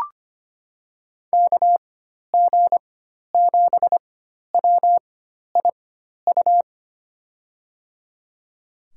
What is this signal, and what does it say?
Morse code, heard as a keyed tone, about 25 wpm, spaced as for 12 wpm, 700 Hz. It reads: KG7WIU
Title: USA Call Signs - 6 Characters - 4x Character Spacing - 25wpm